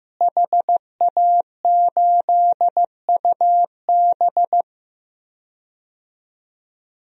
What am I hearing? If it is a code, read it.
Morse code, heard as a keyed tone, about 15 wpm, 700 Hz. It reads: HA8UB
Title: DXCC-100 Call Signs - 5 Characters - Repeated 3x - 15wpm